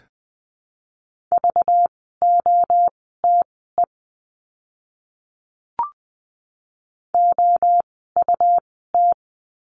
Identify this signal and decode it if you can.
Morse code, heard as a keyed tone, about 20 wpm, spaced as for 15 wpm, 700 Hz. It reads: VOTE OUT
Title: 192 Lesson 31f - Letter v - Words 20wpm